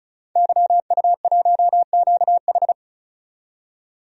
Morse code, heard as a keyed tone, about 36 wpm, 700 Hz. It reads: YU1QH
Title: DXCC-100 Call Signs - 5 Characters - 35wpm